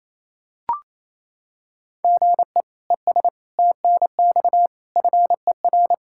Morse code, heard as a keyed tone, about 27 wpm, 700 Hz. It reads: GE ES TNX FER
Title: K1USN SST QSO Practice 28wpm